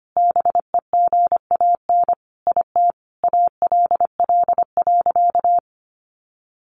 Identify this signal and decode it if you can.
Morse code, heard as a keyed tone, about 25 wpm, 700 Hz. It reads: BEGAN IT ALL.
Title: Sentences from Top 300 Words 25wpm